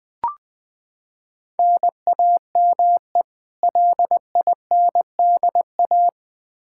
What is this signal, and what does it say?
Morse code, heard as a keyed tone, about 20 wpm, 700 Hz. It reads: NAME LINDA
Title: Practice QSO - Exchange 1 - Beginning - 20wpm